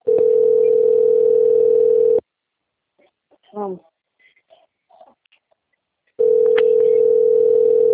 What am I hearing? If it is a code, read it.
Ringback tone (North America), 440 + 480 Hz, with speech.
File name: out- - - - - .28580